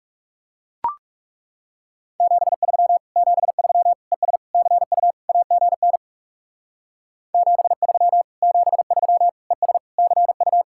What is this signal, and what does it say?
Morse code, heard as a keyed tone, about 39 wpm, 700 Hz. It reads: 73 73 ES CU AGN 73 73 ES CU